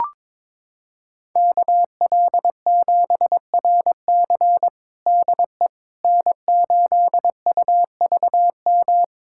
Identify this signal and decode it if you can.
Morse code, heard as a keyed tone, about 22 wpm, 700 Hz. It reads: KL7RC DE N8UVM
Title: USA Callsign DE Callsign - 5 Characters - 22wpm